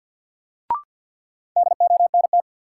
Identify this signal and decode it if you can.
Morse code, heard as a keyed tone, about 52 wpm, 700 Hz. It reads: DONT
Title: Sentences from Top 300 Words - New Word Drill - 50wpm